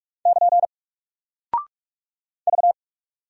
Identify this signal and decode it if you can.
Morse code, heard as a keyed tone, about 48 wpm, 700 Hz. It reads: <KN> V
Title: Single Character (Letter:Number:Punctuation:Prosign) Alt 45wpm